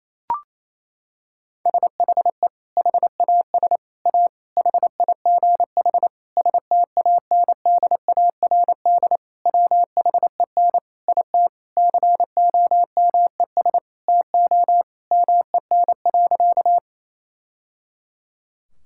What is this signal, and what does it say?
Morse code, heard as a keyed tone, about 27 wpm, 700 Hz. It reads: SHE HAS A HIGH STANDARD WHEN IT COMES TO MEN.
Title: Sentences from Top 1000 Words - Review Part 6 - 28wpm